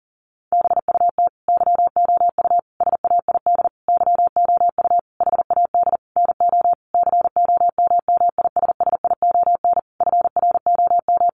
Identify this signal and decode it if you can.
Morse code, heard as a keyed tone, about 40 wpm, 700 Hz. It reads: BUT YOU SAID YOU HAD NO COMMISSION FROM
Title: Sentences from Top 1400 Words 40wpm